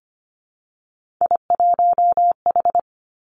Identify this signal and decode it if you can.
Morse code, heard as a keyed tone, about 25 wpm, 700 Hz. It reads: I1H
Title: Three Challenging Letter-Number Combinations 25wpm